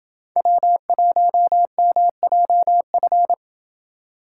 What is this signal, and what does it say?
Morse code, heard as a keyed tone, about 28 wpm, 700 Hz. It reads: W1MJF